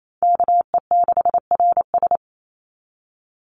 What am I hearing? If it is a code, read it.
Morse code, heard as a keyed tone, about 28 wpm, 700 Hz. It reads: KE6RS
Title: NA SOTA Chaser Callsigns 28wpm